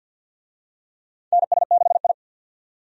Morse code, heard as a keyed tone, about 46 wpm, 700 Hz. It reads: NS6I
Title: USA Call Signs - 4 Characters - 50wpm